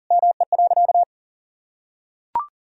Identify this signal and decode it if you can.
Morse code, heard as a keyed tone, about 40 wpm, 700 Hz. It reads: ME.
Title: Sentences from Top 1000 Words - Review Part 2 - 40wpm